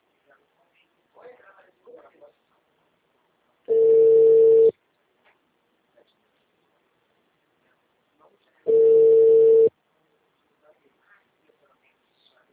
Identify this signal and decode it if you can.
Ringback tone (China, 450 Hz), with speech.